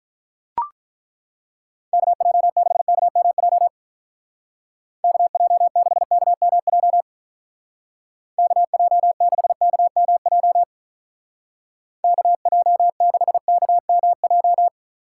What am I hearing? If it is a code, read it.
Morse code, heard as a keyed tone, about 43 wpm, 700 Hz. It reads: KJ6KMJ KJ6KMJ KJ6KMJ KJ6KMJ
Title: Speed Racing - USA 2x3 Call Signs 35wpm